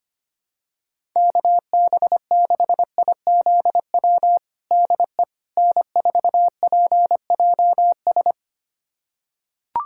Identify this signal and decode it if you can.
Morse code, heard as a keyed tone, about 25 wpm, 700 Hz. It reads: KB6IZW DE N4PJS
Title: USA Callsign DE Callsign 25wpm